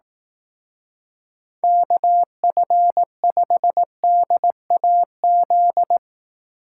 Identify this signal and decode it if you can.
Morse code, heard as a keyed tone, about 18 wpm, 700 Hz. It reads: KF5DAZ
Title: Speed Racing - USA 2x3 Call Signs 15wpm